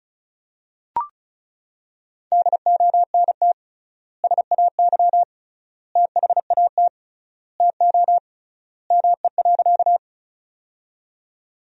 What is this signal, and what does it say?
Morse code, heard as a keyed tone, about 36 wpm, 700 Hz. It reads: DONT SAY THAT TO ME.
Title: 3x Word-Spacing - Sentences from Top 300 Words - 35wpm